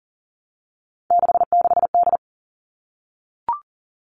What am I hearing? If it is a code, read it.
Morse code, heard as a keyed tone, about 40 wpm, 700 Hz. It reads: 66D
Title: All Cut Numbers - 3-Digit 40wpm